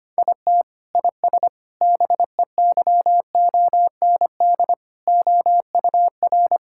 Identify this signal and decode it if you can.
Morse code, heard as a keyed tone, about 25 wpm, 700 Hz. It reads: IT IS BEYOND OUR
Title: Sentences from Top 1000 Words - Review Part 6 - 25wpm